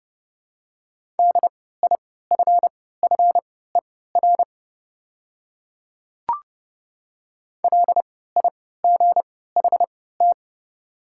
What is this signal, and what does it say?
Morse code, heard as a keyed tone, about 29 wpm, spaced as for 17 wpm, 700 Hz. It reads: DIFFER LIGHT
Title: 3x Character Spacing - Top 200 Words - 30wpm